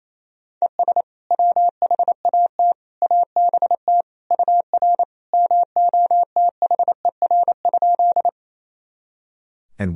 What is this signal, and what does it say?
Morse code, heard as a keyed tone, about 27 wpm, 700 Hz. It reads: ES WHAT ABT UR MOTHER?
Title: Sentences from Top 200 Words and Abbreviations 28wpm